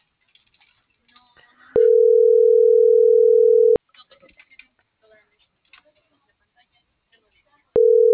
Ringback tone (North America), 440 + 480 Hz.